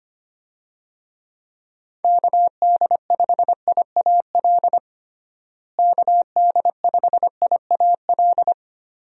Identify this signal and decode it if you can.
Morse code, heard as a keyed tone, about 25 wpm, 700 Hz. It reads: KD5IAL KD5IAL
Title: USA Call Signs - 6 Characters - Repeated 2x - 25wpm